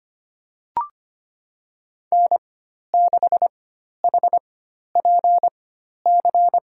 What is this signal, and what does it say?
Morse code, heard as a keyed tone, about 25 wpm, spaced as for 12 wpm, 700 Hz. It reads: N6HPC
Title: USA Call Signs - 4x Character Spacing - 25wpm